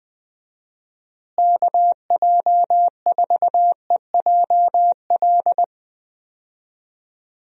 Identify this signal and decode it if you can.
Morse code, heard as a keyed tone, about 20 wpm, 700 Hz. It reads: KJ4EJL
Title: USA Call Signs - 6 Characters - Repeated 2x - 20wpm